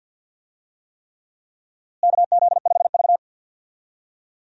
Morse code, heard as a keyed tone, about 51 wpm, 700 Hz. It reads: KZ5V
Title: USA Call Signs - 4 Characters - Repeated 3x - 50wpm